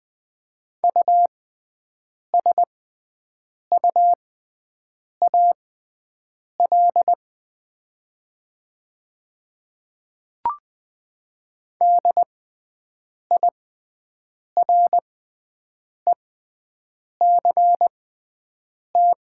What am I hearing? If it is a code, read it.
Morse code, heard as a keyed tone, about 20 wpm, spaced as for 7 wpm, 700 Hz. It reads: USUAL DIRECT
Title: 6x Character Spacing - Top 500 Words - 20wpm